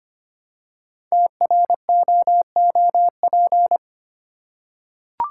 Morse code, heard as a keyed tone, about 25 wpm, 700 Hz. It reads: TROOP
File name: 5 Letter Words - First 3 Letters Given - 25wpm